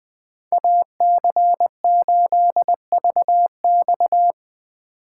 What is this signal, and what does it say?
Morse code, heard as a keyed tone, about 21 wpm, 700 Hz. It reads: AC8VX
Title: USA Call Signs - 5 Characters - 2x Character Spacing - 20wpm